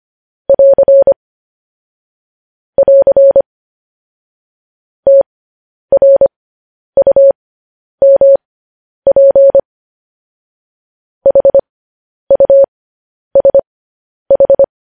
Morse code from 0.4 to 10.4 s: <AR> <AR> TRUMP

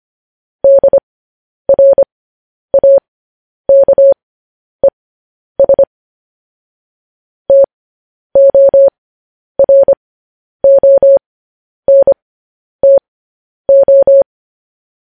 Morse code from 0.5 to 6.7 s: DRAKES

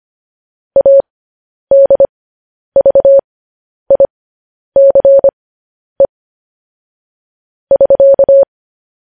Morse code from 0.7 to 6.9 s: ADVICE

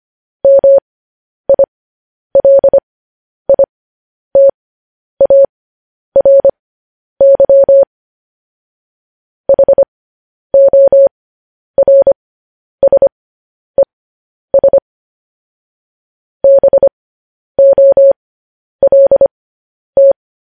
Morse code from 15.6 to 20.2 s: BOLT

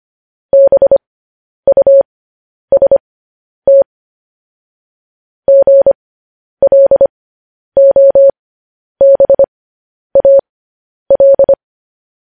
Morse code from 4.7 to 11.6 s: GLOBAL

